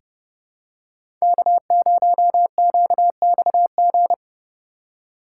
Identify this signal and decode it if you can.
Morse code, heard as a keyed tone, about 32 wpm, 700 Hz. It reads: K0QXG
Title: USA Call Signs - 5 Characters - 3x Character Spacing - 30wpm